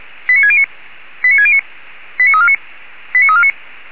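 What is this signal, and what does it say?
Five-tone selective call (ZVEI), heard as quick runs of five tones, tones of about 70 ms. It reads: 89609 89609 89349 89349
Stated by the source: Five-tone ZVEI selective call received on SDR.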